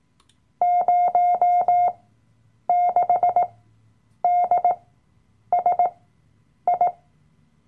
Morse code, heard as a keyed tone, about 18 wpm, spaced as for 8 wpm, 710 Hz. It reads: 06DSI
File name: Workshop Dateien / CW-Online-Kurs / Lektion 007